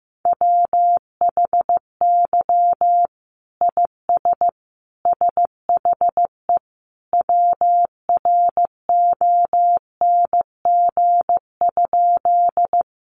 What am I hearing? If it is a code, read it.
Morse code, heard as a keyed tone, about 15 wpm, 700 Hz. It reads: WHY IS SHE WRONG?
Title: Sentences from Top 900 Words 15wpm